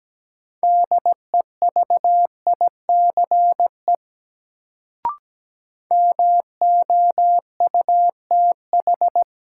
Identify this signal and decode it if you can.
Morse code, heard as a keyed tone, about 17 wpm, 700 Hz. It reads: DEVICE MOUTH
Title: Sentences from Top 1500 Words - New Word Drill Alt - 17wpm